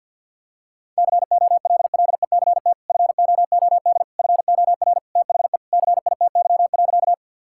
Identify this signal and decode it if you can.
Morse code, heard as a keyed tone, about 50 wpm, 700 Hz. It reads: COLLECT FOOD FOR THE CITY.
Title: Sentences from Top 1600 Words 50wpm